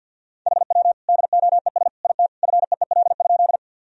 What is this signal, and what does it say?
Morse code, heard as a keyed tone, about 49 wpm, 700 Hz. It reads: HW DOES IT FEEL?
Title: Sentences from Top 400 Words and Abbreviations 50wpm